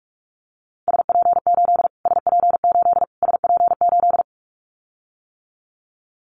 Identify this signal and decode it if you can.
Morse code, heard as a keyed tone, about 44 wpm, 700 Hz. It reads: SP8 SP8 SP8
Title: Callsign Prefixes for Top 25 DXCC - Repeated 3x - 45wpm